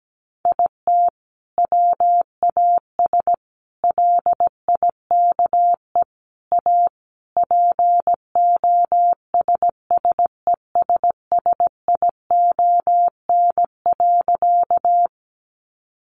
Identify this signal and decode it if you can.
Morse code, heard as a keyed tone, about 17 wpm, 700 Hz. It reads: IT WAS LIKE A POSSESSION.
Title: Sentences from Top 1400 Words 17wpm